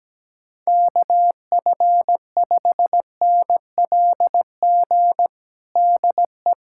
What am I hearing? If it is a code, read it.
Morse code, heard as a keyed tone, about 17 wpm, 700 Hz. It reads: KF5NLG DE
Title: USA Callsign DE Callsign - 6 Characters - 17wpm